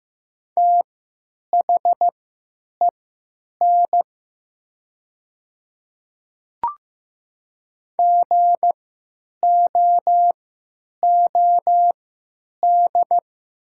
Morse code, heard as a keyed tone, about 15 wpm, spaced as for 9 wpm, 700 Hz. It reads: THEN GOOD
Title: 3x Character Spacing - Top 200 Words - 15wpm